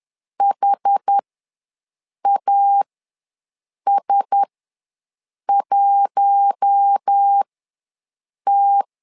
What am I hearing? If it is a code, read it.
Morse code, heard as a keyed tone, about 10 wpm, spaced as for 6 wpm, 800 Hz. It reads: HAS1T
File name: LEZIONI / 4 Q1S3 ws 10wpm 2437